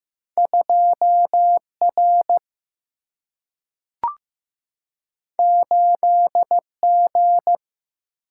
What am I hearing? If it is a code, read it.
Morse code, heard as a keyed tone, about 15 wpm, 700 Hz. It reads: 2R 8G
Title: Single-Letter Single-Number 15wpm